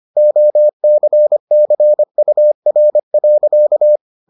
Morse code, heard as a keyed tone, about 25 wpm, 600 Hz. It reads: OCCUR.